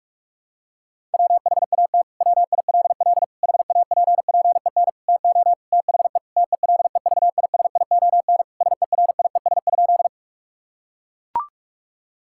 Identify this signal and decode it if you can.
Morse code, heard as a keyed tone, about 41 wpm, 700 Hz. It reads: WHAT WILL HAPPEN TO THE TELEVISION SERIES?
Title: Sentences from Top 1000 Words - Review Part 5 - 45wpm